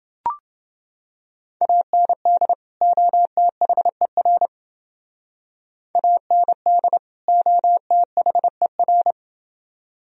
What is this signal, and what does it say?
Morse code, heard as a keyed tone, about 28 wpm, 700 Hz. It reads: AND OTHER AND OTHER